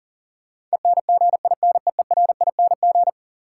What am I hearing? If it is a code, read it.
Morse code, heard as a keyed tone, about 39 wpm, 700 Hz. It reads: ENGINEERING
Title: Sentences from Top 1500 Words - New Word Drill Alt - 40wpm